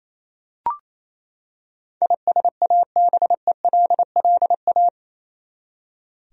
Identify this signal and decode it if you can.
Morse code, heard as a keyed tone, about 28 wpm, 700 Hz. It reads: ISABELLA
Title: USA Common Names - 28wpm